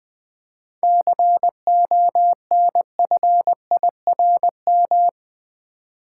Morse code, heard as a keyed tone, about 20 wpm, 700 Hz. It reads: CONFIRM
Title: 7 Letter Words - First 3 Letters Given - 20wpm